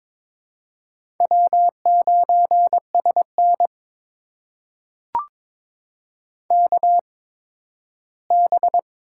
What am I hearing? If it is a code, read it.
Morse code, heard as a keyed tone, about 22 wpm, spaced as for 13 wpm, 700 Hz. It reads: W9SN KB
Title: USA Call Signs - 8x Character Spacing - 22wpm